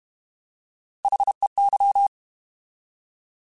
Morse code, heard as a keyed tone, about 32 wpm, 800 Hz. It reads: HEY